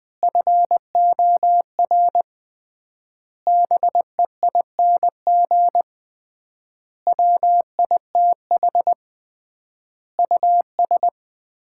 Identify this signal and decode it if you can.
Morse code, heard as a keyed tone, about 20 wpm, 700 Hz. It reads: FOR BEING WITH US
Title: 3x Word-Spacing - Sets of 5 Words - 20wpm